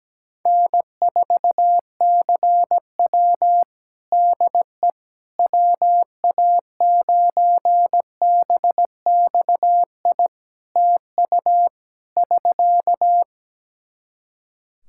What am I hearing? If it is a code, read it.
Morse code, heard as a keyed tone, about 17 wpm, 700 Hz. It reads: N4CW DE WA9BXI TU <SK>
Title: Practice QSO - Exchange 4 - Ending - 17wpm